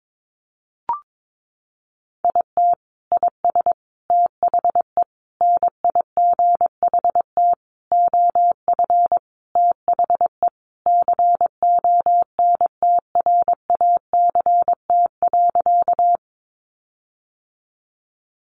Morse code, heard as a keyed tone, about 22 wpm, 700 Hz. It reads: IT IS THE NIGHT OF THE CONTRACT.
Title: Sentences from Top 1400 Words 22wpm